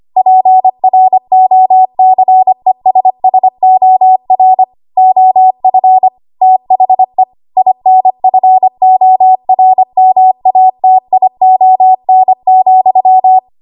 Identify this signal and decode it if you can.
Morse code, heard as a keyed tone, about 25 wpm, 750 Hz. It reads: PROCESSOR OF THE INFORMATION,
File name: Morse / 25WPM